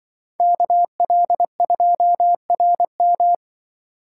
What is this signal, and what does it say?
Morse code, heard as a keyed tone, about 24 wpm, 700 Hz. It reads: KL2RM
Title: Speed Racing - USA 2x2 Call Signs 20wpm